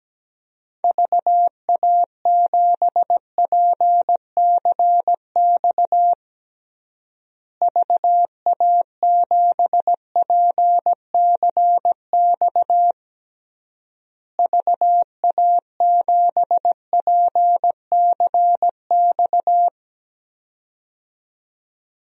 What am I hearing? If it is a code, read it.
Morse code, heard as a keyed tone, about 17 wpm, 700 Hz. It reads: VA7PCX VA7PCX VA7PCX
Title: DXCC-100 Call Signs - 6 Characters - Repeated 3x - 17wpm